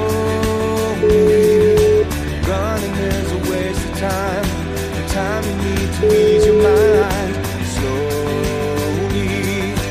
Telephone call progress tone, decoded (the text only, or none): ringback tone